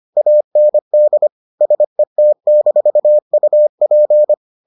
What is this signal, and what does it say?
Morse code, heard as a keyed tone, about 25 wpm, 600 Hz. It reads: AND SET-UP